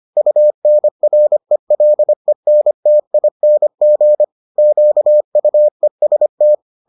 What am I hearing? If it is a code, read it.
Morse code, heard as a keyed tone, about 25 wpm, 600 Hz. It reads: UNRELENTING QUEST